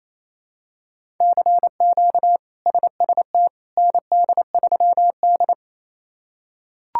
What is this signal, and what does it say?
Morse code, heard as a keyed tone, about 28 wpm, 700 Hz. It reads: CQ SST ND3D
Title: K1USN SST QSO Practice - 28wpm